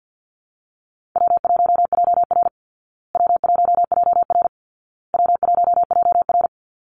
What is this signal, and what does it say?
Morse code, heard as a keyed tone, about 50 wpm, 700 Hz. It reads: W1JR W1JR W1JR
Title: Call Signs Repeated 2x - 50wpm